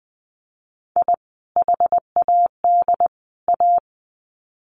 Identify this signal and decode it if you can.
Morse code, heard as a keyed tone, about 20 wpm, 700 Hz. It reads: I HAD A
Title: Sets of 3 Words 20wpm